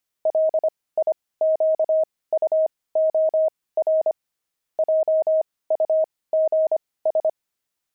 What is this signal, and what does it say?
Morse code, heard as a keyed tone, about 25 wpm, spaced as for 18 wpm, 630 Hz. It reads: LIQUOR JUGS
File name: pangram-fs-18-cs-25-jugs